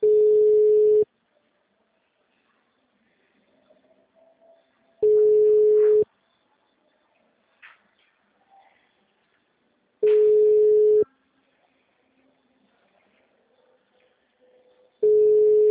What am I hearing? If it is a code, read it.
Ringback tone (Europe, 425 Hz).